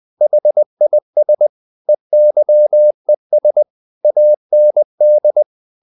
Morse code, heard as a keyed tone, about 20 wpm, 600 Hz. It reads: HIS EYES AND
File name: the old man and the sea part 2 20wpm